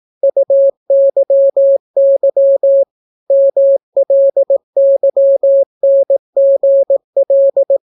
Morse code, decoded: UYY MLYNGL

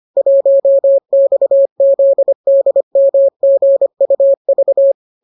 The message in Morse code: 1XZDMGUV

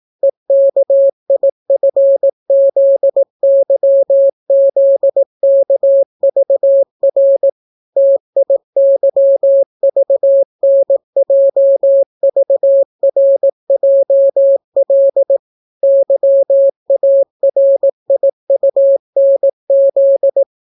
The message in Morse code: EKIFZYZKVR TIYVNJVRJL YARIUNZ

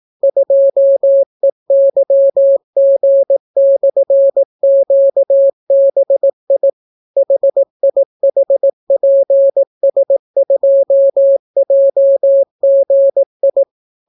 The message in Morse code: 2EYG/QBI HIHPS2JGI

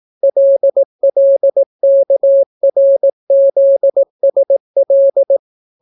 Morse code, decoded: LLKRZSL

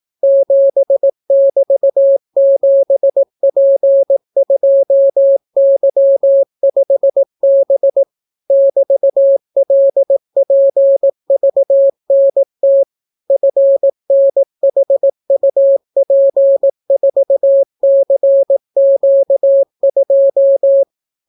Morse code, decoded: 7<BT>7P2Y5B <BT>LPVNT FNHUP4CQ2

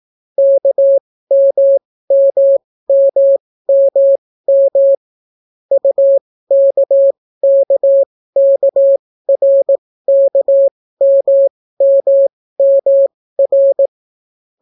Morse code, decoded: KMMMMM UKKKRKMMMR

